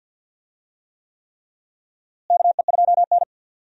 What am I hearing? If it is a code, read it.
Morse code, heard as a keyed tone, about 51 wpm, 700 Hz. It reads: KE2N
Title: USA Call Signs - 4 Characters - Repeated 3x - 50wpm